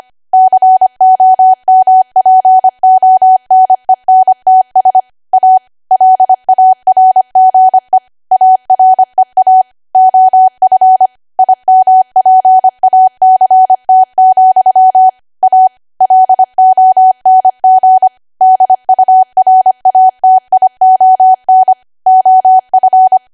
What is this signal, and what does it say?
Morse code, heard as a keyed tone, about 25 wpm, 750 Hz. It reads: COMPONENTS A LARGE AREA OF IMPACT, A LONG DURATION OF
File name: Archive / 25 WPM / 25WPM